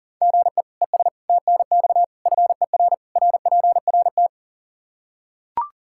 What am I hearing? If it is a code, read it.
Morse code, heard as a keyed tone, about 40 wpm, 700 Hz. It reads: GE ES TNX FER RPRT
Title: K1USN SST QSO Practice 40wpm